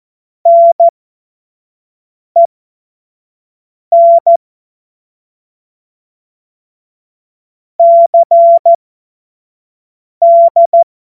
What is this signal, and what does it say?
Morse code, heard as a keyed tone, about 14 wpm, spaced as for 5 wpm, 680 Hz. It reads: NEN CD
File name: Morsekurs / Witze / Witz 12. Oktober 2021, 25 F70